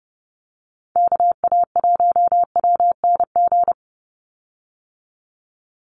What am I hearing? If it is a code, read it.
Morse code, heard as a keyed tone, about 30 wpm, 700 Hz. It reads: KA1WNG
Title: USA Call Signs 30wpm